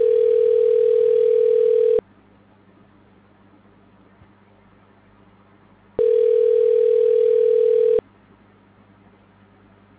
Ringback tone (North America), 440 + 480 Hz.